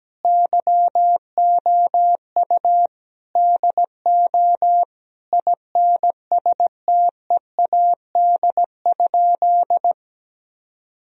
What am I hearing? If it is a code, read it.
Morse code, heard as a keyed tone, about 17 wpm, 700 Hz. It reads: YOU DO INSTEAD?
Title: Sentences from Top 1000 Words - Review Part 4 - 17wpm